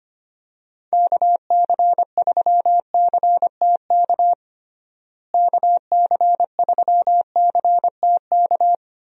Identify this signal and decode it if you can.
Morse code, heard as a keyed tone, about 25 wpm, 700 Hz. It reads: KC3CTK KC3CTK
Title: USA Call Signs - 6 Characters - Repeated 3x - 25wpm